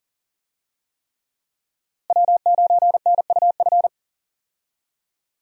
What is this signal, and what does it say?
Morse code, heard as a keyed tone, about 40 wpm, 700 Hz. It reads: W9NUF
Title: USA Call Signs - 5 Characters - Repeated 3x - 40wpm